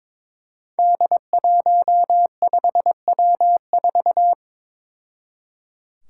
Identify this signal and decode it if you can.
Morse code, heard as a keyed tone, about 22 wpm, 700 Hz. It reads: D15W4
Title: Five Challenging Letter-Number Combinations 22wpm